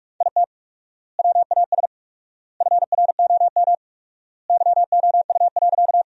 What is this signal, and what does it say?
Morse code, heard as a keyed tone, about 46 wpm, 700 Hz. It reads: IT WAS FROM YOU.
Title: 4x Word-Spacing - Sentences from Top 100 Words - 45wpm